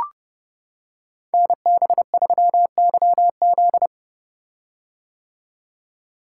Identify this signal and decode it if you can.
Morse code, heard as a keyed tone, about 30 wpm, 700 Hz. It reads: NB3YZ
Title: Code Groups - 5 Letter:Numbers - 30wpm